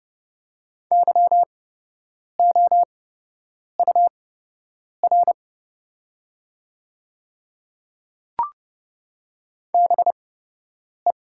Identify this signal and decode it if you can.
Morse code, heard as a keyed tone, about 30 wpm, spaced as for 8 wpm, 700 Hz. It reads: YOUR BE
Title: 8x Character Spacing - Top 100 Words - 30wpm